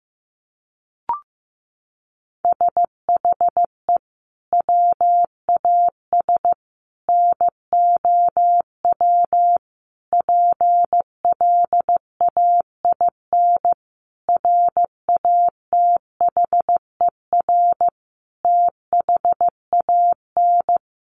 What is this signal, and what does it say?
Morse code, heard as a keyed tone, about 15 wpm, 700 Hz. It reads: SHE WAS NOW PLAIN RATHER THAN
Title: Sentences from Top 1000 Words 15wpm